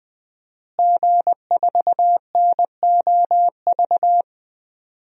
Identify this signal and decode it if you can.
Morse code, heard as a keyed tone, about 20 wpm, 700 Hz. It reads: G4NOV
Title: DXCC-100 Call Signs - 5 Characters - Repeated 2x - 20wpm